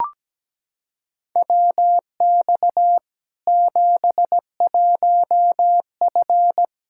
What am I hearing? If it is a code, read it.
Morse code, heard as a keyed tone, about 17 wpm, 700 Hz. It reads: WX 71F
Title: K1USN SST QSO Practice 17wpm